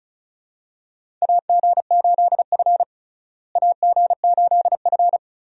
Morse code, heard as a keyed tone, about 36 wpm, 700 Hz. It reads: AG8F AG8F
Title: USA Call Signs - 4 Characters - Repeated 3x - 35wpm